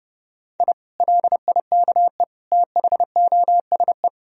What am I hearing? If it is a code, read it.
Morse code, heard as a keyed tone, about 32 wpm, 700 Hz. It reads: I LIKE THOSE
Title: Sentences from Top 1000 Words - Review Part 1 - 30wpm